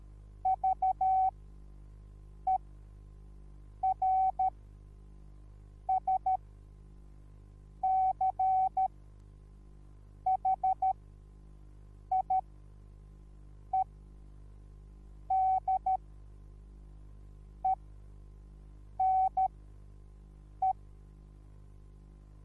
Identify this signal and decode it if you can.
Morse code, heard as a keyed tone, about 13 wpm, spaced as for 5 wpm, 760 Hz. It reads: VERSCHIEDENE
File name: Onlinelehrgang / morsen / DJ4UF Morsen21